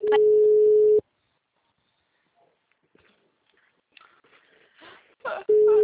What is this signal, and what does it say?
Ringback tone (Europe, 425 Hz).